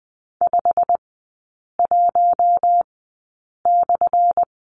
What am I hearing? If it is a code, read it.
Morse code, heard as a keyed tone, about 20 wpm, spaced as for 8 wpm, 700 Hz. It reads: 51/
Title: 212 Lesson 34e - Number 8 - Three Characters 20wpm